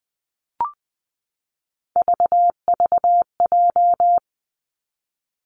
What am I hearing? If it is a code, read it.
Morse code, heard as a keyed tone, about 20 wpm, 700 Hz. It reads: VVJ
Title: Three Challenging Letter-Number Combinations 20wpm